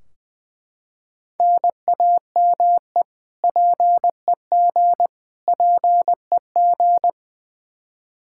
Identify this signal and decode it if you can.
Morse code, heard as a keyed tone, about 20 wpm, 700 Hz. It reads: NAME PEG PEG
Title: Practice QSO - Exchange 1 - Beginning - 20wpm